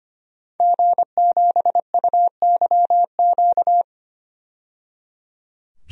Morse code, heard as a keyed tone, about 25 wpm, 700 Hz. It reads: G7UYQ
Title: DXCC-100 Call Signs - 5 Characters - 25wpm